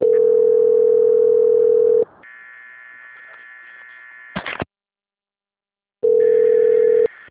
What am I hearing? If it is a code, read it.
Ringback tone (North America), 440 + 480 Hz.